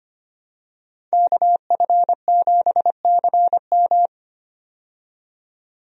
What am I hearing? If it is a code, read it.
Morse code, heard as a keyed tone, about 25 wpm, 700 Hz. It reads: KF7CM
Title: USA Call Signs - 5 Characters - 25wpm